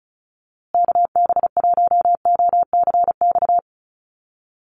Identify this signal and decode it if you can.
Morse code, heard as a keyed tone, about 34 wpm, 700 Hz. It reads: KB1OCX
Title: USA Call Signs Repeated 3x 35wpm